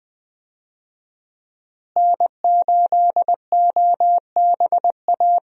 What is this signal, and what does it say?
Morse code, heard as a keyed tone, about 20 wpm, 700 Hz. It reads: N8OBA
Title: USA Call Signs - 5 Characters - Repeated 3x - 20wpm